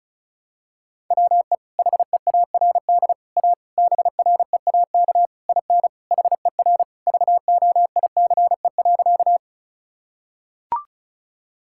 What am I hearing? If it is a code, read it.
Morse code, heard as a keyed tone, about 36 wpm, 700 Hz. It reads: WE HEARD A BREAK IN HER VOICE.
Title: Sentences from Top 1000 Words - Review Part 6 - 35wpm